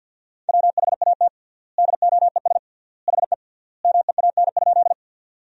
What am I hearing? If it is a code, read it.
Morse code, heard as a keyed tone, about 50 wpm, 700 Hz. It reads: WHAT DOES HE MEAN?
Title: 3x Word-Spacing - Sentences from Top 200 Words - 50wpm